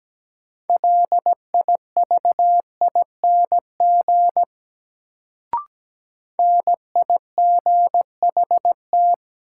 Morse code, heard as a keyed tone, about 17 wpm, 700 Hz. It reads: LIVING NIGHT
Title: Sets of 4 Words - New Word Drill - 17wpm